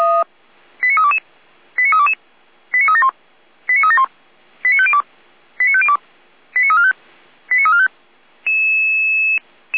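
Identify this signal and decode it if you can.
Five-tone selective call (ZVEI), heard as quick runs of five tones, tones of about 70 ms. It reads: 89322 89322 89371 89371 89502 89502 89345 89345
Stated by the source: Five-tone ZVEI selective call 89230, received by SDR, from feuerwehr-orlamuende.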